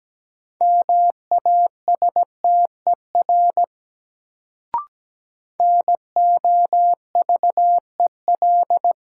Morse code, heard as a keyed tone, about 17 wpm, 700 Hz. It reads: MASTER NOVEL
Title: Sentences from Top 1300 Words - New Word Drill Alt - 17wpm